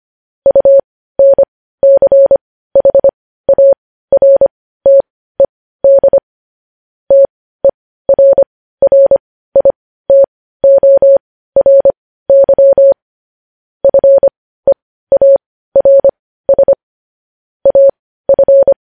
Morse code, heard as a keyed tone, about 25 wpm, spaced as for 15 wpm, 550 Hz. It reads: UNCHARTED TERRITORY FEARS AF
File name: podcast / News headlines-1x-25x15